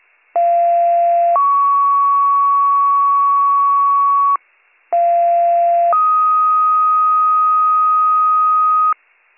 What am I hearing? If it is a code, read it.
Two-tone page, heard as a one-second tone, then a three-second tone, about 690 then 1120 Hz, about 690 then 1250 Hz. It is heard from 0.3 to 8.9 s.